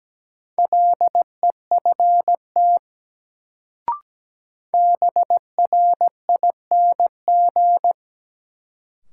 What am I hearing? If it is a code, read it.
Morse code, heard as a keyed tone, about 17 wpm, 700 Hz. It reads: LEFT BRING
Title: Sets of 5 Words Encore - New Word Drill - 17wpm